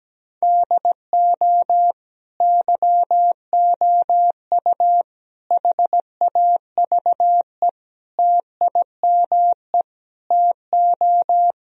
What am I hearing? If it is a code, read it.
Morse code, heard as a keyed tone, about 17 wpm, 700 Hz. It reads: DO YOU HAVE TIME TO
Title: Sentences from Top 1000 Words - Review Part 4 - 17wpm